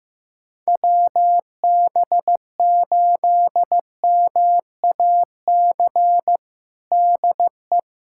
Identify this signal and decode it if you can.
Morse code, heard as a keyed tone, about 15 wpm, 700 Hz. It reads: WB8MAC DE
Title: USA Callsign DE Callsign 15wpm